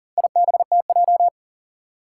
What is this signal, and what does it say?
Morse code, heard as a keyed tone, about 42 wpm, 700 Hz. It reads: IBTJ
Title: Four Challenging Letter-Number Combinations 40wpm